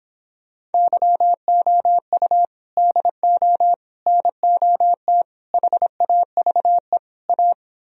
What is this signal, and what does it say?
Morse code, heard as a keyed tone, about 29 wpm, 700 Hz. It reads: YOU DO NOT HAVE A